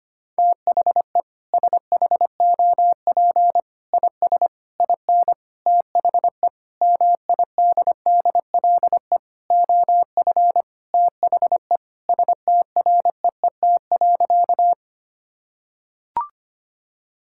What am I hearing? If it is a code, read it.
Morse code, heard as a keyed tone, about 25 wpm, 700 Hz. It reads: THE SHOP IS IN THE MIDDLE OF THE STREET.
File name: practice / 25wpm / Sentences with 4-Word Phrases - 25wpm